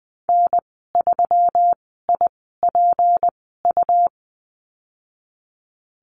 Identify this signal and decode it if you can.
Morse code, heard as a keyed tone, about 20 wpm, spaced as for 15 wpm, 700 Hz. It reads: N3IPU
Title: 167 Lesson 27g - Number 7 - Call Signs 20wpm